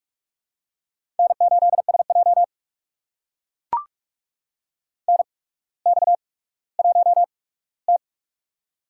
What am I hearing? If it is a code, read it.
Morse code, heard as a keyed tone, about 44 wpm, spaced as for 19 wpm, 700 Hz. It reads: N8SJ NX1T
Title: USA Call Signs - 4 Characters - 8x Character Spacing - 45wpm